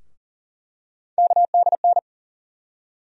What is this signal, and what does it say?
Morse code, heard as a keyed tone, about 40 wpm, 700 Hz. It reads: KDN
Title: USA Call Sign Postfixes - 3 Letters 40wpm